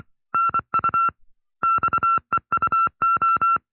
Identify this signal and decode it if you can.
Morse code, heard as a keyed tone, about 24 wpm, 1350 Hz. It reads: NU XEUO
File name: FreeFiles / CW-practice / 20WPM-side1